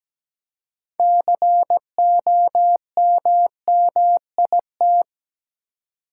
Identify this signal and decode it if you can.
Morse code, heard as a keyed tone, about 17 wpm, 700 Hz. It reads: COMMIT
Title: Six Letter Words - 17wpm